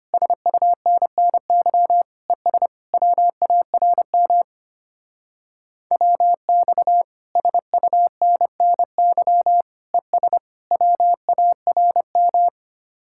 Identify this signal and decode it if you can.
Morse code, heard as a keyed tone, about 32 wpm, 700 Hz. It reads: SUNNY ES WARM WX SUNNY ES WARM